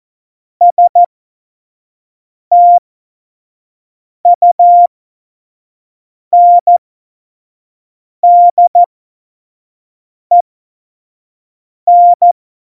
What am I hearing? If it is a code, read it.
Morse code, heard as a keyed tone, about 14 wpm, spaced as for 5 wpm, 700 Hz. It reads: STUNDEN
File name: Morsekurs / Witze / Witz 17 April 2018, 25 F70